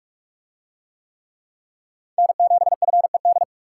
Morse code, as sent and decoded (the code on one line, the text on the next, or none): -. --... ..-. -..
N7FED